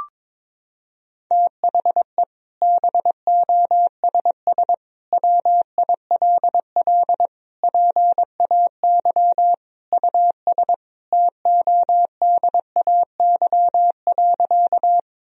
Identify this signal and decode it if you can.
Morse code, heard as a keyed tone, about 22 wpm, 700 Hz. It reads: THE BOSS WILL PAY US TODAY.
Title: Sentences from Top 1000 Words - Review Part 4 - 22wpm